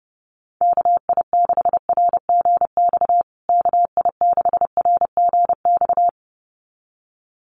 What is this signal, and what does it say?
Morse code, heard as a keyed tone, about 30 wpm, 700 Hz. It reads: KI6RGX KI6RGX
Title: USA Call Signs Repeated 2x 30wpm